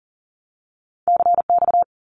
Morse code, heard as a keyed tone, about 40 wpm, 700 Hz. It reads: CX
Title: Abbreviations Repeated 3x - 40wpm